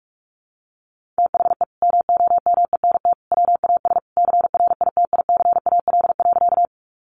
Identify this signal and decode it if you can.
Morse code, heard as a keyed tone, about 44 wpm, 700 Hz. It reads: THE MOMENT WAS CRITICAL.
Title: Sentences from Top 1100 Words 45wpm